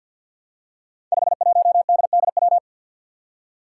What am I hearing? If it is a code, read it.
Morse code, heard as a keyed tone, about 52 wpm, 700 Hz. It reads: 51DDW
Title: Code Groups - 5 Letter:Numbers - 50wpm